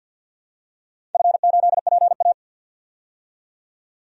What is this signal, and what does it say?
Morse code, heard as a keyed tone, about 48 wpm, 700 Hz. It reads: F8PA
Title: DXCC-100 Call Signs - 4 Characters - 50wpm